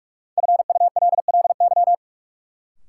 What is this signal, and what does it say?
Morse code, heard as a keyed tone, about 41 wpm, 700 Hz. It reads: FULLY
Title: Sentences from Top 1300 Words - New Word Drill Alt - 45wpm